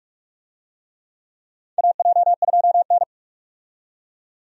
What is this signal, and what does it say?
Morse code, heard as a keyed tone, about 46 wpm, 700 Hz. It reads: AJ2N
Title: USA Call Signs - 4 Characters - 45wpm